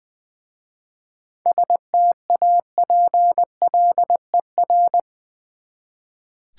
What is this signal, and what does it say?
Morse code, heard as a keyed tone, about 20 wpm, 700 Hz. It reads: STAPLER
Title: Easy Words with Hints - 20wpm